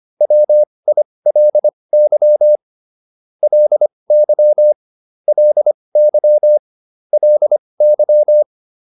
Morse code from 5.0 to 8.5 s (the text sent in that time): LY LY